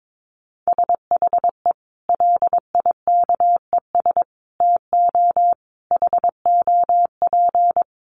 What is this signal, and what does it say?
Morse code, heard as a keyed tone, about 22 wpm, 700 Hz. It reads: SHE LIKES TO HOP
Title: Easy Phrases 22wpm